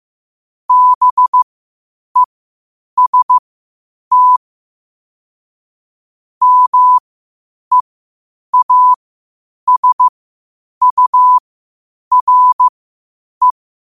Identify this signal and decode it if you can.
Morse code, heard as a keyed tone, about 15 wpm, spaced as for 8 wpm, 1000 Hz. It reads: BEST MEASURE
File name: qotdfiles / 8wpm